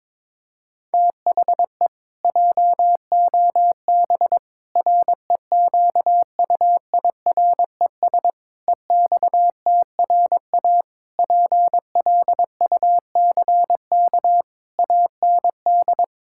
Morse code, heard as a keyed tone, about 22 wpm, 700 Hz. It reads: THE JOB REQUIRES EXTRA PLUCK AND
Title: Pangram Challenge v2 - 22wpm